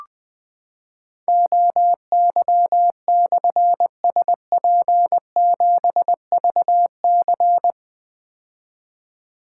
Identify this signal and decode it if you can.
Morse code, heard as a keyed tone, about 20 wpm, 700 Hz. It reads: OY/SP7VC
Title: Difficult Call Signs 20wpm